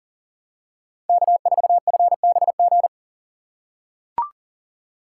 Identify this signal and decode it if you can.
Morse code, heard as a keyed tone, about 40 wpm, 700 Hz. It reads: K4FBG
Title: USA Call Signs - 5 Characters - 2x Character Spacing - 40wpm